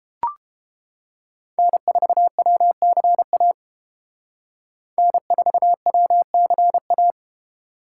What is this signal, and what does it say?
Morse code, heard as a keyed tone, about 28 wpm, 700 Hz. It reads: N4WCA N4WCA